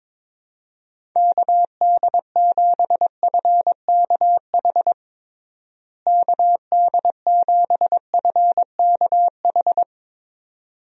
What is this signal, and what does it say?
Morse code, heard as a keyed tone, about 22 wpm, 700 Hz. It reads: KD7FKH KD7FKH
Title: USA Call Signs - 6 Characters - Repeated 2x - 22wpm